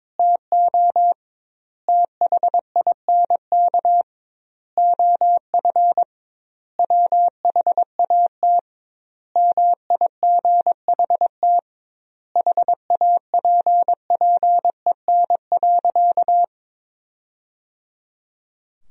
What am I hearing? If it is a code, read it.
Morse code, heard as a keyed tone, about 22 wpm, 700 Hz. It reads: TO THINK OF WHAT MIGHT HAPPEN.
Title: 2x Word-Spacing - Sentences from Top 400 Words - 22wpm